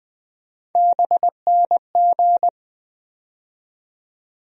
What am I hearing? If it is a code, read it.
Morse code, heard as a keyed tone, about 20 wpm, 700 Hz. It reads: BNG
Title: USA Call Sign Postfixes - 3 Letters 20wpm